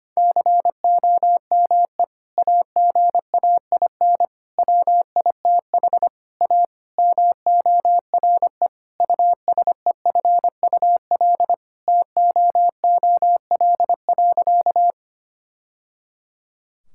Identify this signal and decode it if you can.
Morse code, heard as a keyed tone, about 25 wpm, 700 Hz. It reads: COME AGAIN WITH A MORE USEFUL TOOL.
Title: Sentences from Top 1000 Words - Review Part 6 - 25wpm